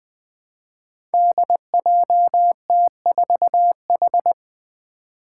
Morse code, heard as a keyed tone, about 20 wpm, 700 Hz. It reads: DJT4H
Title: Five Challenging Letter-Number Combinations 20wpm